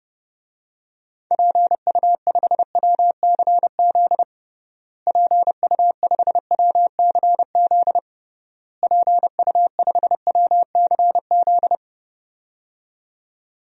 Morse code, heard as a keyed tone, about 30 wpm, 700 Hz. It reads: PU5WCZ PU5WCZ PU5WCZ
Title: DXCC-100 Call Signs - 6 Characters - Repeated 3x - 30wpm